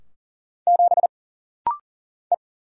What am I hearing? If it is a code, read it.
Morse code, heard as a keyed tone, about 40 wpm, 700 Hz. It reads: 7 E
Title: Single Letter-Number - Rapid-Fire 40wpm